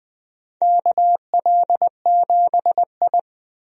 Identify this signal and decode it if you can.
Morse code, heard as a keyed tone, about 20 wpm, 700 Hz. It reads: KL7I
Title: DXCC-100 Call Signs - 4 Characters - 20wpm